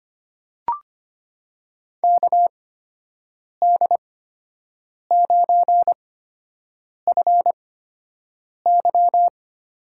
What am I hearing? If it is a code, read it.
Morse code, heard as a keyed tone, about 25 wpm, spaced as for 7 wpm, 700 Hz. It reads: KD9FY
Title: USA Call Signs - 6 Characters - 8x Character Spacing - 25wpm